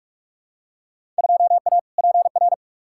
Morse code, heard as a keyed tone, about 42 wpm, 700 Hz. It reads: 2A PR
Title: Field Day Exchanges Encore - 45wpm